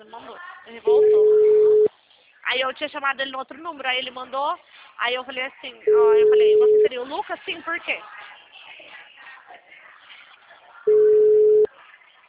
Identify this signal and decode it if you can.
Ringback tone (Europe, 425 Hz), with speech.
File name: out- -2014- - - .1694